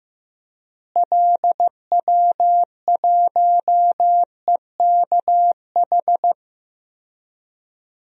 Morse code, heard as a keyed tone, about 16 wpm, 700 Hz. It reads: LW1EKH
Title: DXCC-100 Call Signs - 6 Characters - Repeated 2x - 15wpm